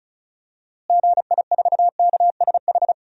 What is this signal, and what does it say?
Morse code, heard as a keyed tone, about 34 wpm, 700 Hz. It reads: GI4KSH
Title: DXCC-100 Call Signs - 6 Characters - Repeated 2x - 35wpm